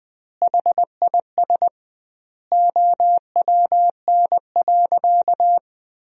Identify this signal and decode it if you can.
Morse code, heard as a keyed tone, about 21 wpm, 700 Hz. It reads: HIS OWN.
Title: 2x Word-Spacing - Sentences from Top 300 Words - 20wpm